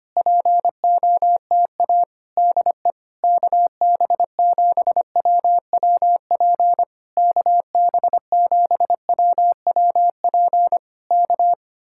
Morse code, heard as a keyed tone, about 26 wpm, 700 Hz. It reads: POTA DE KB7WWP KB7WWP K
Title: Speed-Racing Common QSO Phrases - 25wpm